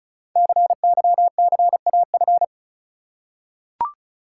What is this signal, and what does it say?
Morse code, heard as a keyed tone, about 35 wpm, 690 Hz. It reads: CYCAF
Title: Code Groups - 5 Letters - 35wpm